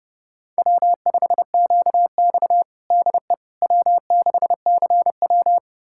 Morse code, heard as a keyed tone, about 30 wpm, 700 Hz. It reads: W5QX DE W6CW
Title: USA Callsign DE Callsign - 4 Characters - 30wpm